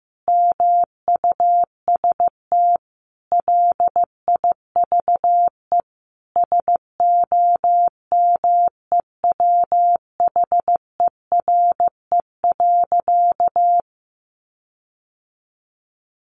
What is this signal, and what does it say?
Morse code, heard as a keyed tone, about 15 wpm, 700 Hz. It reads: MUST LIVE SOMEWHERE.
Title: Sentences from Top 1400 Words 15wpm